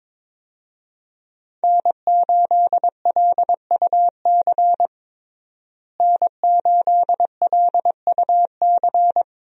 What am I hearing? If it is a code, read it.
Morse code, heard as a keyed tone, about 21 wpm, 700 Hz. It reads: N8LUC N8LUC
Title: USA Call Signs - 5 Characters - Repeated 2x - 22wpm